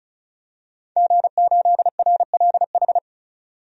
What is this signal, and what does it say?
Morse code, heard as a keyed tone, about 36 wpm, 700 Hz. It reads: G8RLH
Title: DXCC-100 Call Signs - 5 Characters - 35wpm